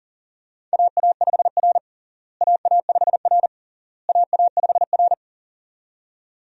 Morse code, heard as a keyed tone, about 40 wpm, 700 Hz. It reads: AA5R AA5R AA5R
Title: USA Call Signs - 4 Characters - Repeated 3x - 40wpm